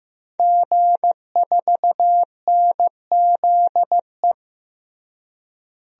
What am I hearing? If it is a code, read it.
Morse code, heard as a keyed tone, about 15 wpm, 700 Hz. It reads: G4NZE
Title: DXCC-100 Call Signs - 5 Characters - Repeated 2x - 15wpm